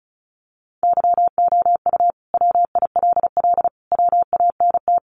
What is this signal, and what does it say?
Morse code, heard as a keyed tone, about 34 wpm, 700 Hz. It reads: YOU WILL WANT
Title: Sentences from Top 1300 Words 35wpm